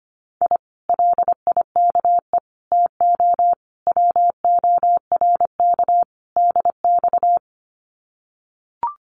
Morse code, heard as a keyed tone, about 25 wpm, 700 Hz. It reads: I LIKE TO WORK DX
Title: Common QSO Phrases Encore 25wpm